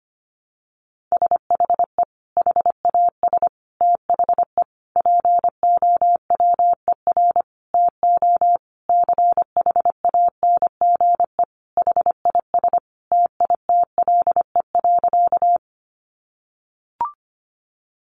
Morse code, heard as a keyed tone, about 25 wpm, 700 Hz. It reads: SHE HAS THE POWER TO CHANGE HIS TITLE.
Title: Sentences from Top 1200 Words 25wpm